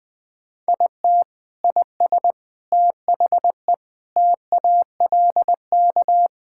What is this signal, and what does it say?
Morse code, heard as a keyed tone, about 21 wpm, 700 Hz. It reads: IT IS THE TALK
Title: Sentences with 4-Word Phrases - Spoken Beginning - 20wpm